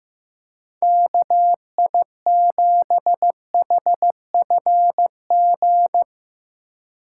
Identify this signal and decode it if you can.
Morse code, heard as a keyed tone, about 16 wpm, 700 Hz. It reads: KI7HFG
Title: Speed Racing - USA Call Signs 15wpm